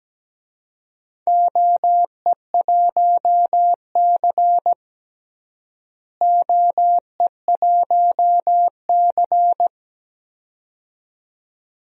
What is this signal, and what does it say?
Morse code, heard as a keyed tone, about 17 wpm, 700 Hz. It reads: OE1C OE1C
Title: DXCC-100 Call Signs - 4 Characters - Repeated 2x - 17wpm